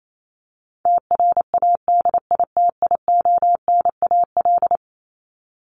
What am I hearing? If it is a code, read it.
Morse code, heard as a keyed tone, about 28 wpm, 700 Hz. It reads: TRADITIONAL
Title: Top 751-1000 Words 28wpm